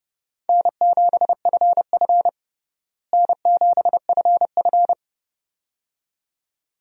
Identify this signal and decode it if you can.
Morse code, heard as a keyed tone, about 30 wpm, 700 Hz. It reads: N7FF N7FF
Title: USA Call Signs - 4 Characters - Repeated 2x - 30wpm